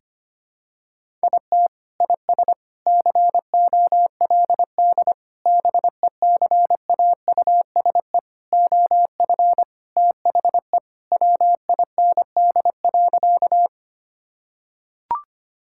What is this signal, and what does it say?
Morse code, heard as a keyed tone, about 25 wpm, 700 Hz. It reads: IT IS COLD BECAUSE OF THE WIND.
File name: practice / 25wpm / Sentences with 3-Word Phrases - 25wpm